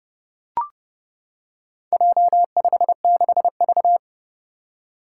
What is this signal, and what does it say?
Morse code, heard as a keyed tone, about 30 wpm, 700 Hz. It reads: J56V
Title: Four Challenging Letter-Number Combinations 30wpm